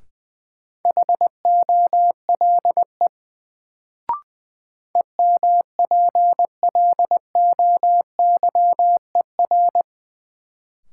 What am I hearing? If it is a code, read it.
Morse code, heard as a keyed tone, about 20 wpm, 700 Hz. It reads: HOLE EMPLOYER
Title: Sentences from Top 1600 Words - New Word Drill - 20wpm